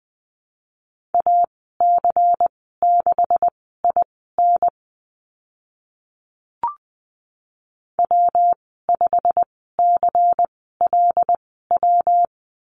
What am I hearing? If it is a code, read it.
Morse code, heard as a keyed tone, about 20 wpm, spaced as for 14 wpm, 700 Hz. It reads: AC6IN W5CLW
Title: 140 Lesson 23g - Letter f - Call Signs 20wpm